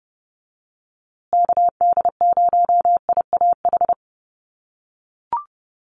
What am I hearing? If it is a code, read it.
Morse code, heard as a keyed tone, about 30 wpm, 700 Hz. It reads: KD0IAH
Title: USA Call Signs Repeated 2x 30wpm